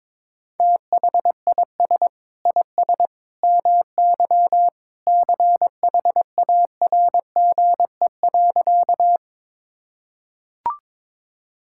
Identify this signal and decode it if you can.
Morse code, heard as a keyed tone, about 22 wpm, 700 Hz. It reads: THIS IS MY CHARGE.
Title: Sentences from Top 1000 Words - Review Part 6 - 22wpm